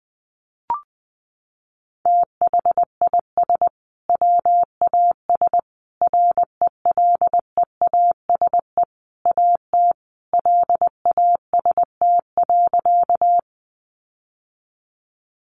Morse code, heard as a keyed tone, about 20 wpm, 700 Hz. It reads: THIS WAS RELEASE AT LAST.
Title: Sentences from Top 1100 Words 20wpm